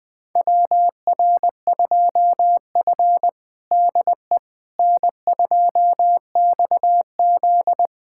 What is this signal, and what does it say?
Morse code, heard as a keyed tone, about 20 wpm, 700 Hz. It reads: WR2F DE N2XZ
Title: USA Callsign DE Callsign - 4 Characters - 20wpm